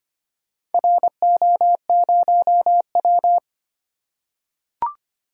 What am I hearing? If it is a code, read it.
Morse code, heard as a keyed tone, about 25 wpm, 700 Hz. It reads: RO0W
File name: practice / 25wpm / Four Letter-Number Combinations - 25wpm